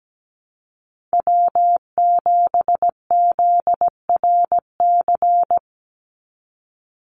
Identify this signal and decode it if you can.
Morse code, heard as a keyed tone, about 17 wpm, 700 Hz. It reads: W7ZRC
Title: Call Signs Repeated 3x - 17wpm